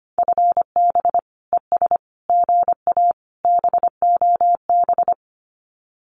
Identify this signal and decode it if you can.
Morse code, heard as a keyed tone, about 25 wpm, 700 Hz. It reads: FB ES GA BOB
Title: Common QSO Phrases Encore 25wpm